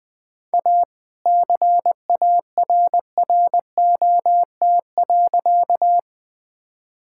Morse code, heard as a keyed tone, about 20 wpm, 700 Hz. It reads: A CARROT.
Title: All-Age Appropriate Jokes - 20wpm